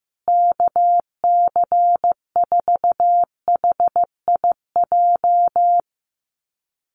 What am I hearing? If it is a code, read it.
Morse code, heard as a keyed tone, about 15 wpm, 700 Hz. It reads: KC4HIJ
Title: USA Call Signs Repeated 2x 15wpm